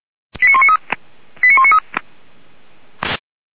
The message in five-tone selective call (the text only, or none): 89193 89193